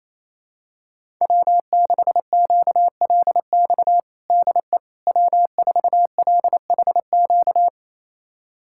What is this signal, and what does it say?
Morse code, heard as a keyed tone, about 27 wpm, 700 Hz. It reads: W6QLX DE W4LHQ
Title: USA Callsign DE Callsign - 5 Characters - 28wpm